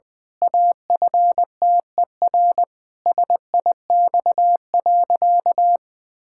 Morse code, heard as a keyed tone, about 20 wpm, 700 Hz. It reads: AFTER SIX.
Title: Pangram Challenge v2 - 20wpm